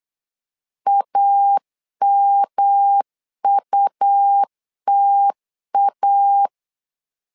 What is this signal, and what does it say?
Morse code, heard as a keyed tone, about 8 wpm, 800 Hz. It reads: AMUTA
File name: LEZIONI / 3 MWRO / 3 M 8wpm 2300